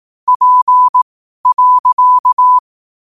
Morse code, heard as a keyed tone, about 18 wpm, spaced as for 12 wpm, 1000 Hz. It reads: P.
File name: qotdfiles / 12wpm